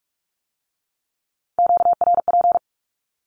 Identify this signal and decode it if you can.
Morse code, heard as a keyed tone, about 46 wpm, 700 Hz. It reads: QRP
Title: Q-Codes with Spoken Meaning - 45wpm